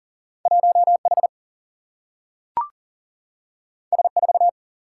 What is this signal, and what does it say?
Morse code, heard as a keyed tone, about 40 wpm, 700 Hz. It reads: 1H S4
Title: Single-Letter Single-Number 40wpm